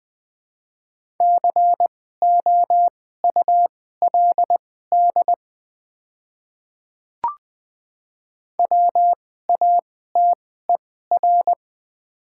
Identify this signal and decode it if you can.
Morse code, heard as a keyed tone, about 20 wpm, spaced as for 14 wpm, 700 Hz. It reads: COULD WATER
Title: 2x Character Spacing - Top 100 Words - 20wpm